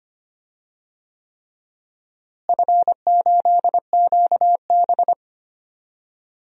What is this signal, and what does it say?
Morse code, heard as a keyed tone, about 25 wpm, 700 Hz. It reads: F8QB
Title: DXCC-100 Call Signs - 4 Characters - Repeated 3x - 25wpm